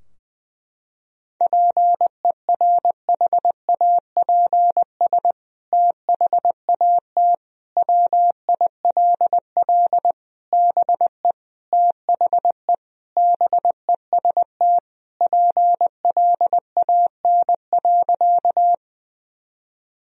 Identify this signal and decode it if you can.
Morse code, heard as a keyed tone, about 21 wpm, 700 Hz. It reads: PERHAPS THAT WILL BE THE BEST PLAN.
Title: Sentences from Top 1000 Words - Review Part 2 - 20wpm